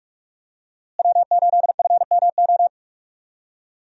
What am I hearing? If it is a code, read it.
Morse code, heard as a keyed tone, about 41 wpm, 700 Hz. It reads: W8FMO
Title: USA Call Signs - 5 Characters - Repeated 2x - 45wpm